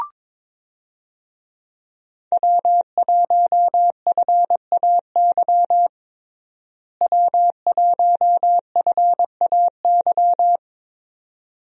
Morse code, heard as a keyed tone, about 21 wpm, 700 Hz. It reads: W1FAY W1FAY
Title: USA Call Signs - 5 Characters - Repeated 2x - 22wpm